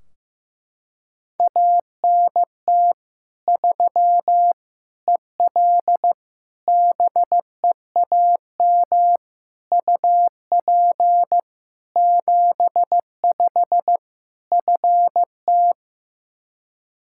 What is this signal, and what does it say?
Morse code, heard as a keyed tone, about 16 wpm, 700 Hz. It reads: ANT 3 EL BEAM UP 75 FT
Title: Practice QSO - Exchange 2 - Info - 15wpm